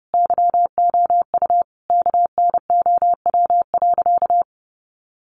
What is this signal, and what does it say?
Morse code, heard as a keyed tone, about 30 wpm, 700 Hz. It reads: YOU KNOW.
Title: Sentences from Top 1300 Words 30wpm